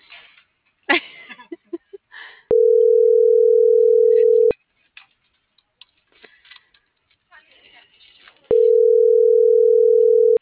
Ringback tone (North America), 440 + 480 Hz.